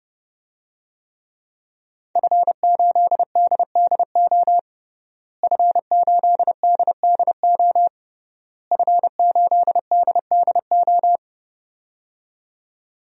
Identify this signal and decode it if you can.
Morse code, heard as a keyed tone, about 29 wpm, 700 Hz. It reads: F8DDO F8DDO F8DDO
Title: DXCC-100 Call Signs - 5 Characters - Repeated 3x - 30wpm